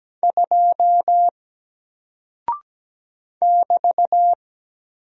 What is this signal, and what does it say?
Morse code, heard as a keyed tone, about 17 wpm, 700 Hz. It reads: 2 <BT>
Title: Single Character (Letter:Number:Punctuation:Prosign) 17wpm